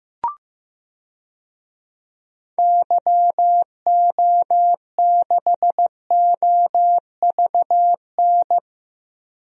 Morse code, heard as a keyed tone, about 15 wpm, 700 Hz. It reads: YO6OVN
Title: DXCC-100 Call Signs - 6 Characters - Repeated 2x - 15wpm